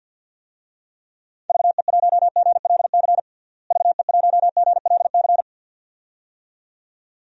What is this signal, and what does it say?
Morse code, heard as a keyed tone, about 50 wpm, 700 Hz. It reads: VE1GLC VE1GLC
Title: DXCC-100 Call Signs - 6 Characters - Repeated 2x - 50wpm